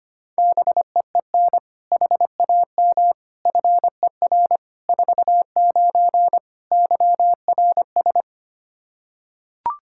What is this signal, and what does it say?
Morse code, heard as a keyed tone, about 25 wpm, 700 Hz. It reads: BEEN HAM FER 49 YRS
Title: K1USN SST QSO Practice 25wpm